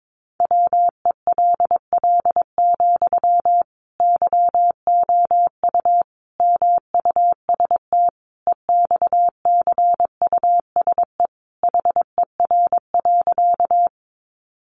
Morse code, heard as a keyed tone, about 22 wpm, 700 Hz. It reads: WELL, YOU MUST EXCUSE HER.
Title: Sentences from Top 700 Words 22wpm